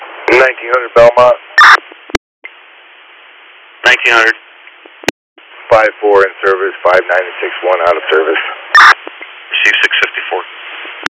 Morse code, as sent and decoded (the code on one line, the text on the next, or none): none